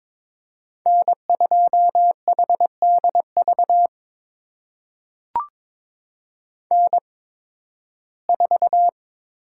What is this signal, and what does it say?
Morse code, heard as a keyed tone, about 22 wpm, spaced as for 14 wpm, 700 Hz. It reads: N2HDV N4
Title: USA Call Signs - 5 Characters - 8x Character Spacing - 22wpm